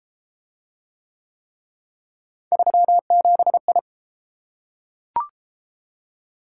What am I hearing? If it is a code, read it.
Morse code, heard as a keyed tone, about 31 wpm, 700 Hz. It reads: 37I